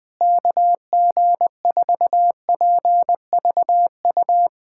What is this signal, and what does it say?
Morse code, heard as a keyed tone, about 20 wpm, 700 Hz. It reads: KG4PVU
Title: USA Callsign DE Callsign - 6 Characters - 20wpm